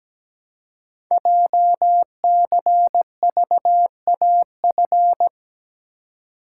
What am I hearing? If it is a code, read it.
Morse code, heard as a keyed tone, about 17 wpm, 700 Hz. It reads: JCVAF
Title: Code Groups - 5 Letters - 17wpm